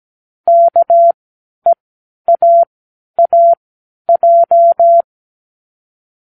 Morse code, heard as a keyed tone, about 17 wpm, spaced as for 10 wpm, 680 Hz. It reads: KEAAJ
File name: k7qo / Lesson18(A-L)